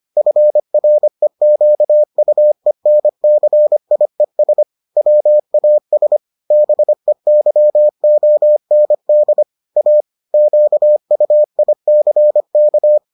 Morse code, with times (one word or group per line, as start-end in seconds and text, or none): 0.2-4.6 FREQUENCIES
5.0-6.2 WAS
6.5-9.4 BEYOND
9.8-10.0 A
10.3-13.0 QUICK